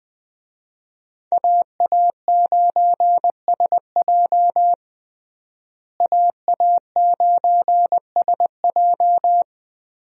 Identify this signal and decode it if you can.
Morse code, heard as a keyed tone, about 20 wpm, 700 Hz. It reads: AA9SJ AA9SJ
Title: USA Call Signs - 5 Characters - Repeated 2x - 20wpm